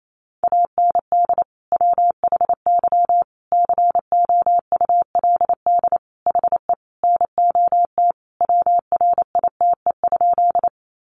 Morse code, heard as a keyed tone, about 28 wpm, 700 Hz. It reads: AND WHY COULD HE NOT WRITE?
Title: Sentences from Top 600 Words 28wpm